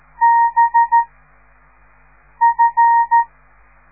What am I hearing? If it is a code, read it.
Morse code, heard as a keyed tone, about 13 wpm, 930 Hz. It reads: B F